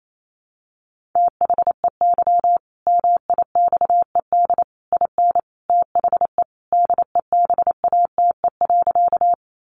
Morse code, heard as a keyed tone, about 28 wpm, 700 Hz. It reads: THEY MIXED IN THE DEBATE.
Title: Sentences from Top 1500 Words 28wpm